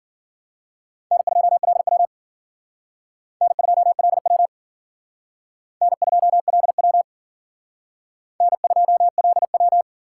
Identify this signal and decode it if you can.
Morse code, heard as a keyed tone, about 51 wpm, 700 Hz. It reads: N2LW N2LW N2LW N2LW